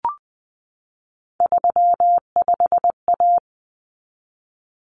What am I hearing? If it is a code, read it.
Morse code, heard as a keyed tone, about 20 wpm, 700 Hz. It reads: 35A
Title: Common Cut Numbers - 3-Digit 20wpm